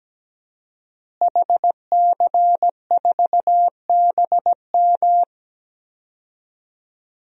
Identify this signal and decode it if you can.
Morse code, heard as a keyed tone, about 17 wpm, 700 Hz. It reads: HC4BM
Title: DXCC-100 Call Signs - 5 Characters - Repeated 3x - 17wpm